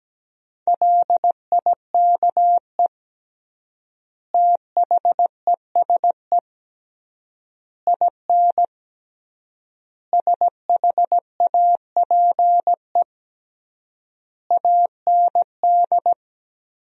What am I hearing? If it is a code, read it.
Morse code, heard as a keyed tone, about 17 wpm, 700 Hz. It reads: LIKE THESE IN SHAPE AND
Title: 3x Word-Spacing - Sentences from Top 500 Words - 17wpm